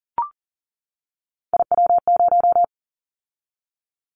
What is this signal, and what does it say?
Morse code, heard as a keyed tone, about 40 wpm, 700 Hz. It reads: IW0
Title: Callsign Prefixes for Top 25 DXCC - 40wpm